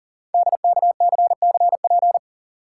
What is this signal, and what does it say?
Morse code, heard as a keyed tone, about 41 wpm, 690 Hz. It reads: DKCCP
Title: Code Groups - 5 Letters - 40wpm